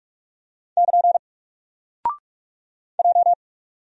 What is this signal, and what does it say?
Morse code, heard as a keyed tone, about 44 wpm, 700 Hz. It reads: <KN> J
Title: Single Character (Letter:Number:Punctuation:Prosign) 45wpm